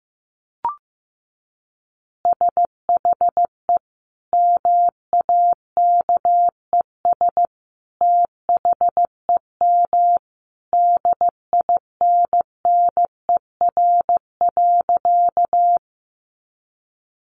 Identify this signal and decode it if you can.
Morse code, heard as a keyed tone, about 15 wpm, 700 Hz. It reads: SHE MAKES THEM DINNER.
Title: Easy Phrases 15wpm